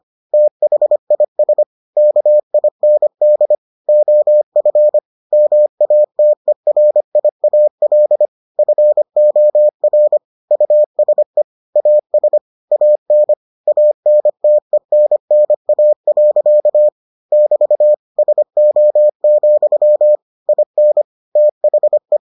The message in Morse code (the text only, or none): THIS KIND OF MATERIAL FOR USE AS AN ANTENNA. <BT> SO, IN THE